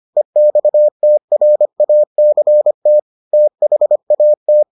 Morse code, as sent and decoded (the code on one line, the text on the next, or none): . -..- - .-. .- -.-. - / - .... .- -
EXTRACT THAT